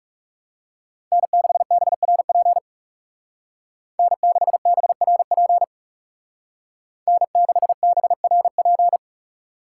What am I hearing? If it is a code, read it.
Morse code, heard as a keyed tone, about 35 wpm, 700 Hz. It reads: N6BRP N6BRP N6BRP
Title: Speed Racing - USA 1x3 Call Signs 30wpm